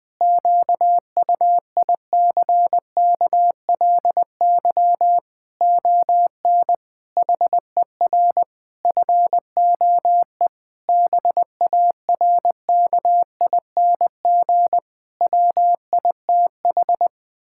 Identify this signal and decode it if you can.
Morse code, heard as a keyed tone, about 21 wpm, 700 Hz. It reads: QUICKLY ON HER FOE BARKING WITH
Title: Pangram Challenge v2 - 20wpm